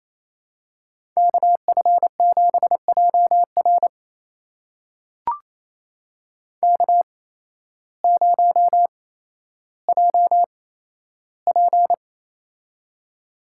Morse code, heard as a keyed tone, about 28 wpm, spaced as for 13 wpm, 700 Hz. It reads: KF7JR K0JP
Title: USA Call Signs - 8x Character Spacing - 28wpm